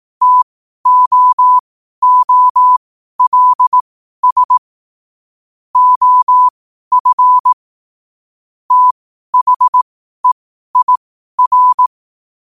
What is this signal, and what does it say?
Morse code, heard as a keyed tone, about 18 wpm, spaced as for 12 wpm, 1000 Hz. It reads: TOOLS OF THEIR
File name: qotdfiles / 12wpm